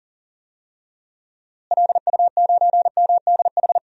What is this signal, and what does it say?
Morse code, heard as a keyed tone, about 40 wpm, 700 Hz. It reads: LU9MDH
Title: DXCC-100 Call Signs - 6 Characters - Repeated 2x - 40wpm